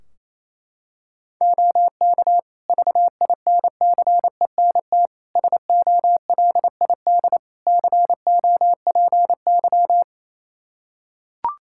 Morse code, heard as a keyed tone, about 28 wpm, 700 Hz. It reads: OK VINCENT SOLID COPY
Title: Practice QSO - Exchange 3 - More Info - 28wpm